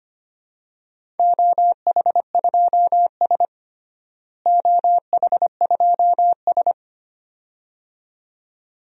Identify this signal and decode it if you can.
Morse code, heard as a keyed tone, about 25 wpm, 700 Hz. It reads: OH2S OH2S
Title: DXCC-100 Call Signs - 4 Characters - Repeated 2x - 25wpm